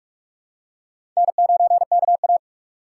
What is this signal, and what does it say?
Morse code, heard as a keyed tone, about 41 wpm, 700 Hz. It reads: N9KA
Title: USA Call Signs - 4 Characters - 3x Character Spacing - 45wpm